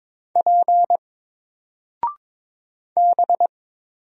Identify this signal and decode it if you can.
Morse code, heard as a keyed tone, about 22 wpm, 700 Hz. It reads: P B
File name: Single-Letter-Number-22wpm